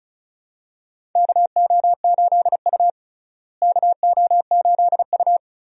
Morse code, heard as a keyed tone, about 36 wpm, 700 Hz. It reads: KO8U KO8U
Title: USA Call Signs - 4 Characters - Repeated 3x - 35wpm